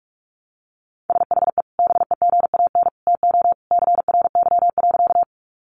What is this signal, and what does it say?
Morse code, heard as a keyed tone, about 46 wpm, 700 Hz. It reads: SHE BEGAN TO CRY.
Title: Sentences from Top 500 Words 45wpm